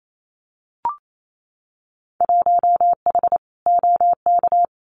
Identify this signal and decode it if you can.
Morse code, heard as a keyed tone, about 28 wpm, 700 Hz. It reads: 1H OK
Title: Winter Field Day Exchanges 28wpm